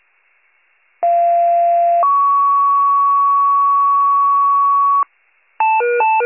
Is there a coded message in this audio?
Yes: two-tone page from 1.0 to 5.0 s, heard as a one-second tone, then a three-second tone, about 690 then 1120 Hz, with noise.